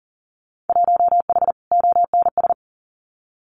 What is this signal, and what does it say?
Morse code, heard as a keyed tone, about 38 wpm, 700 Hz. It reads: 1H ONS
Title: Winter Field Day Exchanges 40wpm